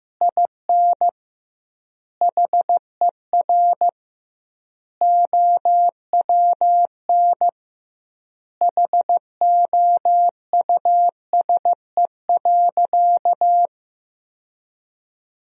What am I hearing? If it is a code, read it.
Morse code, heard as a keyed tone, about 15 wpm, 700 Hz. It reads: IN HER OWN HOUSE.
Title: 2x Word-Spacing - Sentences from Top 300 Words - 15wpm